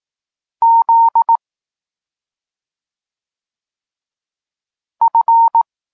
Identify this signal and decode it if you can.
Morse code, heard as a keyed tone, about 18 wpm, 920 Hz. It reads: Z F